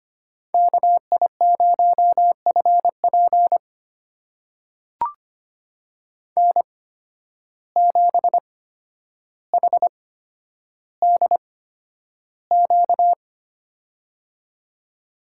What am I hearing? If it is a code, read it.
Morse code, heard as a keyed tone, about 25 wpm, spaced as for 11 wpm, 700 Hz. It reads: KI0FP N7HDQ
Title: USA Call Signs - 8x Character Spacing - 25wpm